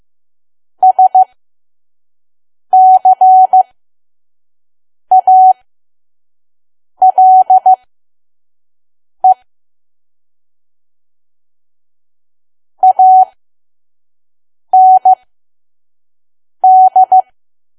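Morse code, heard as a keyed tone, about 15 wpm, spaced as for 5 wpm, 750 Hz. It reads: SCALE AND
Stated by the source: Morse code practice at 5 WPM from KQ2P.COM.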